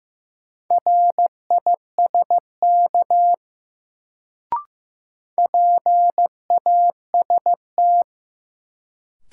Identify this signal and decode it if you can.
Morse code, heard as a keyed tone, about 15 wpm, 700 Hz. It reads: RISK PAST
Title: Sentences from Top 600 Words - New Word Drill - 15wpm